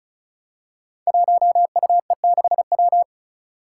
Morse code, heard as a keyed tone, about 36 wpm, 700 Hz. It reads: 1UE6W
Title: Five Challenging Letter-Number Combinations 35wpm